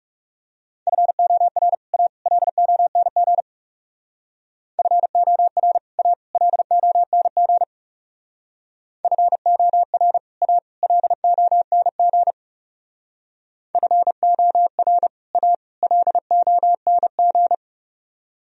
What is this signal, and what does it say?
Morse code, heard as a keyed tone, about 37 wpm, 700 Hz. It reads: FOR A LONG FOR A LONG FOR A LONG FOR A LONG
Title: Speed-Racing - Sets of 3 Words 30wpm